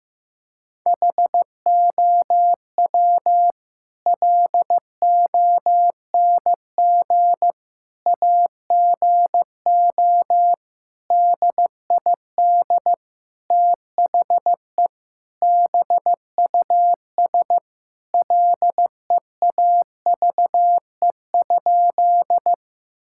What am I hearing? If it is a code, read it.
Morse code, heard as a keyed tone, about 15 wpm, 700 Hz. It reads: HOW LONG AGO DID THE BUS LEAVE?
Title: Sentences from Top 1000 Words - Review Part 6 - 15wpm